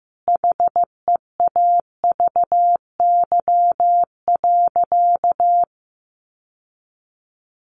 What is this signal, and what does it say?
Morse code, heard as a keyed tone, about 15 wpm, 700 Hz. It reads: HEAVY.
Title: Sentences from Top 1100 Words 15wpm